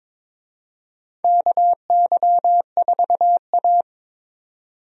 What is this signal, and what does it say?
Morse code, heard as a keyed tone, about 22 wpm, 700 Hz. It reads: KY4A
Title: USA Call Signs - 4 Characters - 22wpm